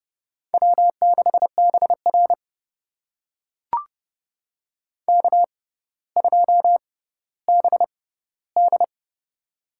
Morse code, heard as a keyed tone, about 30 wpm, spaced as for 15 wpm, 700 Hz. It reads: W6BR K2BD
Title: USA Call Signs - 6x Character Spacing - 30wpm